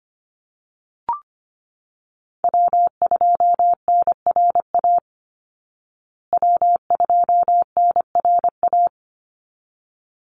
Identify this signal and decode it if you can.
Morse code, heard as a keyed tone, about 25 wpm, 700 Hz. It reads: W2NRA W2NRA
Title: Call Signs Repeated 3x - 25wpm